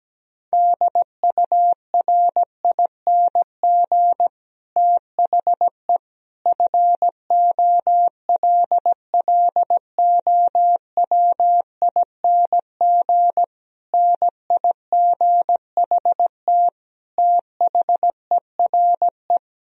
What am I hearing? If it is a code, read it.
Morse code, heard as a keyed tone, about 17 wpm, 700 Hz. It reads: DURING THE FOLLOWING NIGHT THERE
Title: Sentences from Top 1000 Words - Review Part 4 - 17wpm